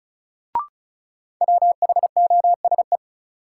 Morse code, heard as a keyed tone, about 34 wpm, 700 Hz. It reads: WHOSE
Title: Sentences from Top 1200 Words - New Word Drill - 35wpm